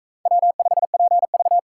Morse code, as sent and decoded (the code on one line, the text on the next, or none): .-- ..... .--. ...-
W5PV